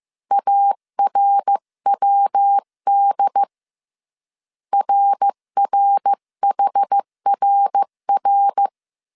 Morse code, heard as a keyed tone, about 15 wpm, 800 Hz. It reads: ARWD RRHRR
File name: CW / DET / 1 3 1500C S A 14wpm 2447